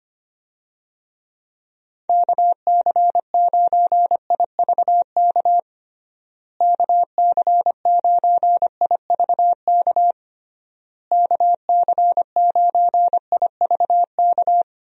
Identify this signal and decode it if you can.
Morse code, heard as a keyed tone, about 25 wpm, 700 Hz. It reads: KC9IVK KC9IVK KC9IVK
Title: USA Call Signs - 6 Characters - Repeated 3x - 25wpm